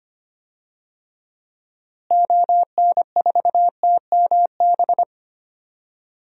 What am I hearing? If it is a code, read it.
Morse code, heard as a keyed tone, about 25 wpm, 700 Hz. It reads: ON4TMB
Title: DXCC-100 Call Signs - 6 Characters - Repeated 2x - 25wpm